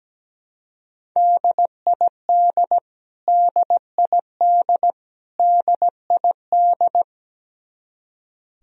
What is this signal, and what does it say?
Morse code, heard as a keyed tone, about 17 wpm, 700 Hz. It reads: DID DID DID
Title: Top 100 Words Repeated 17wpm